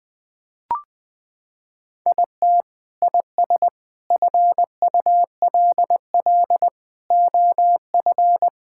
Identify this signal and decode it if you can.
Morse code, heard as a keyed tone, about 20 wpm, 700 Hz. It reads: IT IS FULL OF
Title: Sentences from Top 1000 Words - Review Part 5 - 20wpm